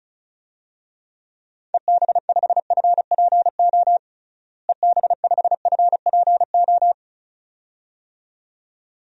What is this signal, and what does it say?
Morse code, heard as a keyed tone, about 34 wpm, 700 Hz. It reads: EB5FPO EB5FPO
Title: DXCC-100 Call Signs - 6 Characters - Repeated 2x - 35wpm